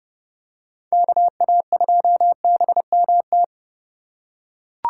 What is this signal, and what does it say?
Morse code, heard as a keyed tone, about 29 wpm, 700 Hz. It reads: KA2BMT